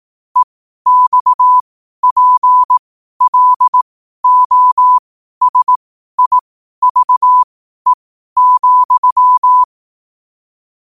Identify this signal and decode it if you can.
Morse code, heard as a keyed tone, about 18 wpm, spaced as for 12 wpm, 1000 Hz. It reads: EXPLOSIVE,
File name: qotdfiles / 12wpm